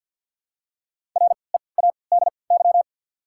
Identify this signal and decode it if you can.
Morse code, heard as a keyed tone, about 54 wpm, spaced as for 29 wpm, 700 Hz. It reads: READY
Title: 3x Character Spacing - Top 500 Words - 50wpm